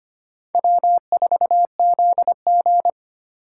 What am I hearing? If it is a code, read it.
Morse code, heard as a keyed tone, about 25 wpm, 700 Hz. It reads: W4ZG
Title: USA Call Signs - 4 Characters - 3x Character Spacing - 25wpm